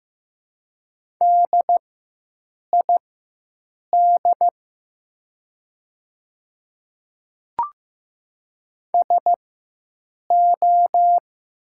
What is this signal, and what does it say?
Morse code, heard as a keyed tone, about 15 wpm, spaced as for 7 wpm, 700 Hz. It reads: DID SO
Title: 4x Character Spacing - Top 100 Words - 15wpm